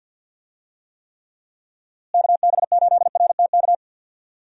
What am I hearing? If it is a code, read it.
Morse code, heard as a keyed tone, about 52 wpm, 700 Hz. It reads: KB8RTK
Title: USA Call Signs - 6 Characters - Repeated 2x - 50wpm